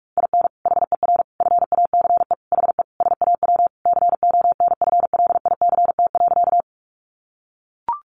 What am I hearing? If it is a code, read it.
Morse code, heard as a keyed tone, about 44 wpm, 700 Hz. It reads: IN HER FACE HE SAW CONFLICT.
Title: Sentences from Top 1300 Words 45wpm